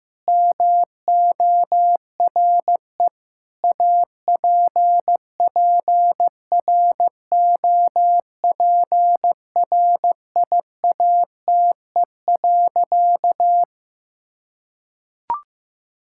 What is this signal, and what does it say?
Morse code, heard as a keyed tone, about 16 wpm, 700 Hz. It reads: MORE APPROPRIATE.
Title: Sentences from Top 1600 Words 15wpm